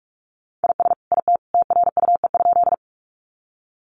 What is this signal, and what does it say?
Morse code, heard as a keyed tone, about 44 wpm, 700 Hz. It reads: IS IT TRUE?
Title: Sentences from Top 400 Words 45wpm